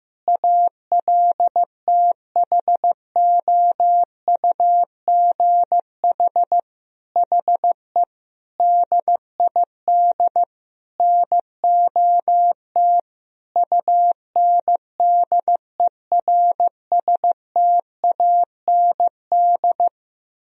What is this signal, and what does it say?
Morse code, heard as a keyed tone, about 15 wpm, 700 Hz. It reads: ALTHOUGH HE DID NOT UNDERSTAND
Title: Sentences from Top 1000 Words - Review Part 3 - 15wpm